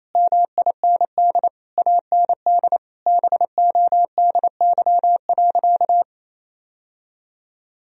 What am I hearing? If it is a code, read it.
Morse code, heard as a keyed tone, about 28 wpm, 700 Hz. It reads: MIND AND BODY.
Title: Sentences from Top 1000 Words - Review Part 2 - 28wpm